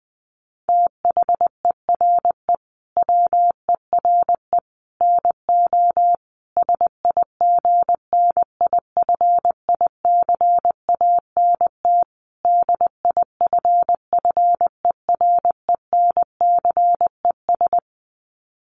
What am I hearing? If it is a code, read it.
Morse code, heard as a keyed tone, about 20 wpm, 700 Hz. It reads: THERE WERE NO SIGNIFICANT DIFFERENCES
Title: Sets of 5 Words Encore2 20wpm